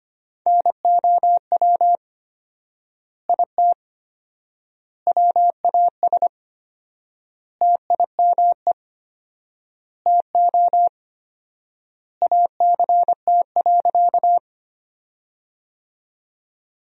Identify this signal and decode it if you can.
Morse code, heard as a keyed tone, about 25 wpm, 700 Hz. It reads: NOW IT WAS TIME TO ACT.
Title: 4x Word-Spacing - Sentences from Top 200 Words - 25wpm